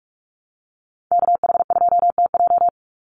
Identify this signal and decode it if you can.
Morse code, heard as a keyed tone, about 44 wpm, 700 Hz. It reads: KH2TJ
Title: NA SOTA Chaser Callsigns 45wpm